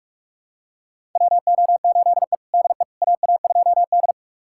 Morse code, heard as a keyed tone, about 43 wpm, 700 Hz. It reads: WO8E DE AA2D
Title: USA Callsign DE Callsign - 4 Characters - 45wpm